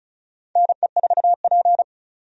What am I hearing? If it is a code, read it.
Morse code, heard as a keyed tone, about 36 wpm, 700 Hz. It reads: NE4P
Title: USA Call Signs - 4 Characters - 35wpm